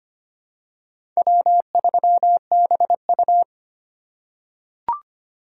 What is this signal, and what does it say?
Morse code, heard as a keyed tone, about 25 wpm, 700 Hz. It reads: W3BU
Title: USA Call Signs - 4 Characters - 3x Character Spacing - 25wpm